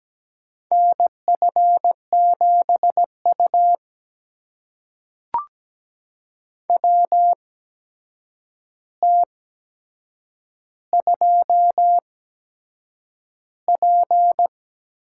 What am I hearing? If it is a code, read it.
Morse code, heard as a keyed tone, about 17 wpm, spaced as for 7 wpm, 700 Hz. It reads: NF7U WT2P
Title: USA Call Signs - 4 Characters - 8x Character Spacing - 17wpm